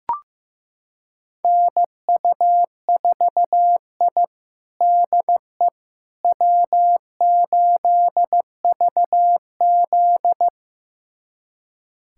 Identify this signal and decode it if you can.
Morse code, heard as a keyed tone, about 16 wpm, 700 Hz. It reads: NU4I DE W8VZ
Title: USA Callsign DE Callsign - 4 Characters - 15wpm